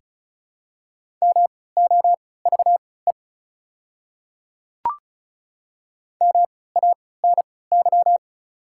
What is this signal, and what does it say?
Morse code, heard as a keyed tone, about 36 wpm, spaced as for 20 wpm, 700 Hz. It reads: MOVE MANY
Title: 3x Character Spacing - Top 200 Words - 35wpm